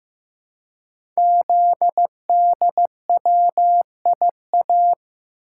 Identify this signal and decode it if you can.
Morse code, heard as a keyed tone, about 15 wpm, 700 Hz. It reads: ZDWIA
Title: Code Groups - 5 Letters - 15wpm